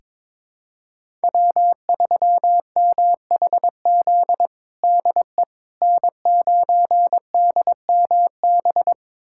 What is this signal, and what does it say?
Morse code, heard as a keyed tone, about 22 wpm, 700 Hz. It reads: W3MHZ DE N9DMB
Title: USA Callsign DE Callsign - 5 Characters - 22wpm